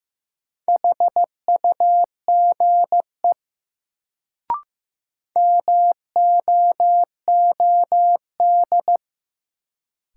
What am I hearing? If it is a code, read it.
Morse code, heard as a keyed tone, about 15 wpm, 700 Hz. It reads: HUGE MOOD
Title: Sentences from Top 1100 Words - New Word Drill Alt - 15wpm